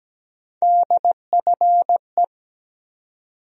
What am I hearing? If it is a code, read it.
Morse code, heard as a keyed tone, about 17 wpm, 700 Hz. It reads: DFE
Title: USA Call Sign Postfixes - 3 Letters 17wpm